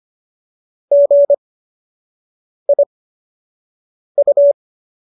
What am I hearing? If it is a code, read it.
Morse code, heard as a keyed tone, about 25 wpm, spaced as for 6 wpm, 570 Hz. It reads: GIU